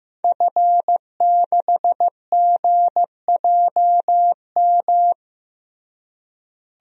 Morse code, heard as a keyed tone, about 16 wpm, 700 Hz. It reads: F6GJM
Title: DXCC-100 Call Signs - 5 Characters - Repeated 3x - 15wpm